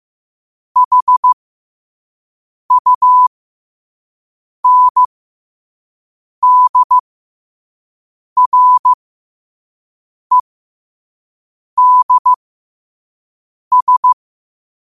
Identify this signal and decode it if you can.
Morse code, heard as a keyed tone, about 15 wpm, spaced as for 5 wpm, 1000 Hz. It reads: HUNDREDS
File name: qotdfiles / 5wpm